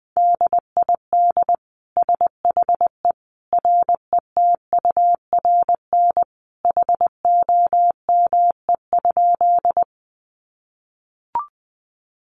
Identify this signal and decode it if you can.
Morse code, heard as a keyed tone, about 20 wpm, 700 Hz. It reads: DID SHE RETURN HOME?
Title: Sentences from Top 800 Words 20wpm